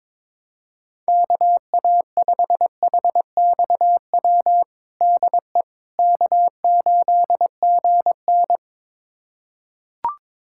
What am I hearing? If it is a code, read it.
Morse code, heard as a keyed tone, about 22 wpm, 700 Hz. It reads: KA5HXW DE K8GN
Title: USA Callsign DE Callsign 22wpm